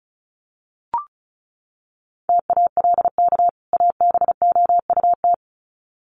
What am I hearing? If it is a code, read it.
Morse code, heard as a keyed tone, about 34 wpm, 700 Hz. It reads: TALK ABOUT
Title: Sets of 2 Words 35wpm